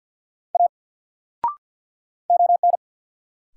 Morse code, heard as a keyed tone, about 49 wpm, 700 Hz. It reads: A ON
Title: Sentences from Top 100 Words - New Word Drill - 50wpm